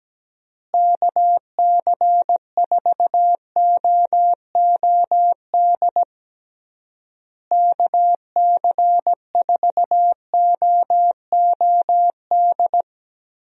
Morse code, heard as a keyed tone, about 17 wpm, 700 Hz. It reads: KC4OOD KC4OOD
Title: USA Call Signs - 6 Characters - Repeated 3x - 17wpm